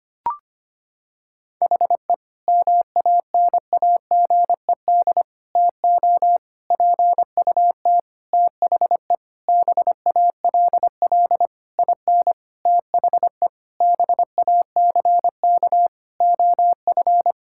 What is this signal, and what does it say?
Morse code, heard as a keyed tone, about 25 wpm, 700 Hz. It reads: HE MANAGED TO PUT THE BALL IN THE BACK OF
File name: practice / 25wpm / Sentences with 9-Word Phrases - 25wpm